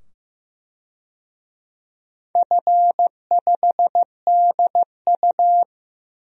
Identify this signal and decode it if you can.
Morse code, heard as a keyed tone, about 15 wpm, 700 Hz. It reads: F5DU
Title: DXCC-100 Call Signs - 4 Characters - 15wpm